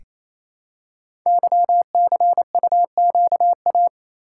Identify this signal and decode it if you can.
Morse code, heard as a keyed tone, about 28 wpm, 690 Hz. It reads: YCUQA
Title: Code Groups - 5 Letters - 28wpm